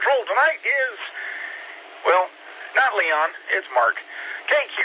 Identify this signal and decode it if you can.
Emergency Alert System (SAME) tones, heard as harsh buzzing bursts, with speech. It is heard from 1.1 to 4.4 s.